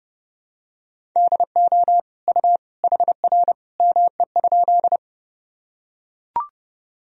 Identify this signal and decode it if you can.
Morse code, heard as a keyed tone, about 30 wpm, 700 Hz. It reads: DO U HR ME?
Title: Sentences from Top 300 Words and Abbreviations 30wpm